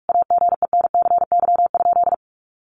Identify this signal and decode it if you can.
Morse code, heard as a keyed tone, about 44 wpm, 700 Hz. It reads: AGENCY?
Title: Sentences from Top 1300 Words 45wpm